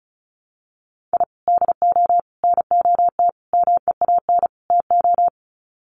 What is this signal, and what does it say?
Morse code, heard as a keyed tone, about 34 wpm, 700 Hz. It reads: I DO NOT MEAN TO
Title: Sets of 5 Words 35wpm